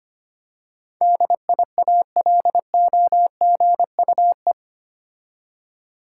Morse code, heard as a keyed tone, about 25 wpm, 700 Hz. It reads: DIALOGUE
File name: 8 Letter Words - First 3 Letters Given - 25wpm